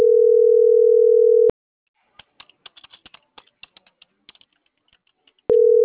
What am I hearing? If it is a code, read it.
Ringback tone (North America), 440 + 480 Hz.